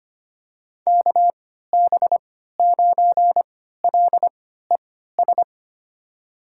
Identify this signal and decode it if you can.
Morse code, heard as a keyed tone, about 25 wpm, spaced as for 14 wpm, 700 Hz. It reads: KB9LES
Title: USA Call Signs - 6 Characters - 3x Character Spacing - 25wpm